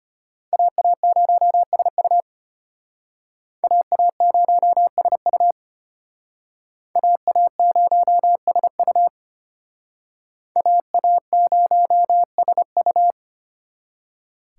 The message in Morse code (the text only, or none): AA0SU AA0SU AA0SU AA0SU